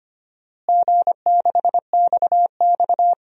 Morse code, heard as a keyed tone, about 25 wpm, 700 Hz. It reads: G6XX
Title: DXCC-100 Call Signs - 4 Characters - Repeated 2x - 25wpm